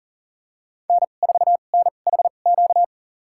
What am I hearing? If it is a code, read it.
Morse code, heard as a keyed tone, about 40 wpm, spaced as for 29 wpm, 700 Hz. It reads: N4NHQ
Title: USA Call Signs - 5 Characters - 2x Character Spacing - 40wpm